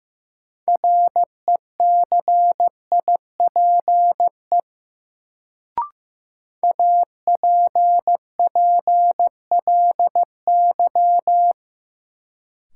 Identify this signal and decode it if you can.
Morse code, heard as a keyed tone, about 15 wpm, 700 Hz. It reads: RECIPE APPLY
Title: Sentences from Top 1200 Words - New Word Drill - 15wpm